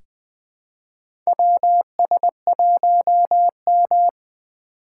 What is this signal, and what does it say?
Morse code, heard as a keyed tone, about 20 wpm, 700 Hz. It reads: WS1M
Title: USA Call Signs - 2x Character Spacing - 20wpm